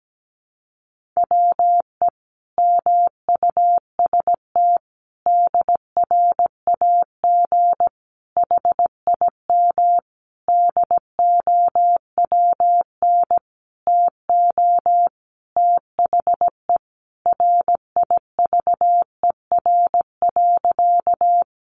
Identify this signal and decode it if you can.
Morse code, heard as a keyed tone, about 17 wpm, 700 Hz. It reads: WE MUST DRAG HIM DOWN TO THE RIVER.
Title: Sentences from Top 1500 Words 17wpm